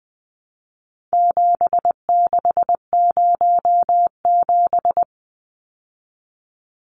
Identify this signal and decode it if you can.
Morse code, heard as a keyed tone, about 20 wpm, 700 Hz. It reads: 7607
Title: Numbers - 4-Digit 20wpm